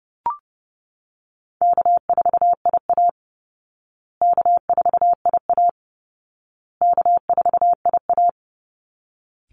Morse code, heard as a keyed tone, about 28 wpm, 700 Hz. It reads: K4IA K4IA K4IA
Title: Easy Call Signs Repeated 3x - 30wpm